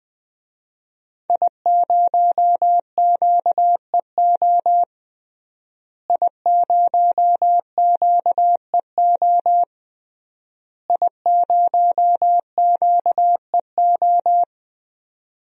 Morse code, heard as a keyed tone, about 21 wpm, 700 Hz. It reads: I0QEO I0QEO I0QEO
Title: DXCC-100 Call Signs - 5 Characters - Repeated 3x - 20wpm